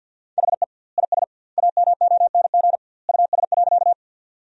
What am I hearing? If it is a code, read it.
Morse code, heard as a keyed tone, about 47 wpm, 700 Hz. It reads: HE IS AMONG US.
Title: 2x Word-Spacing - Sentences from Top 500 Words - 50wpm